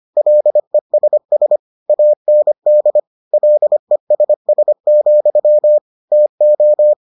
Morse code, heard as a keyed tone, about 25 wpm, 600 Hz. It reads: LESS AND LESS, TO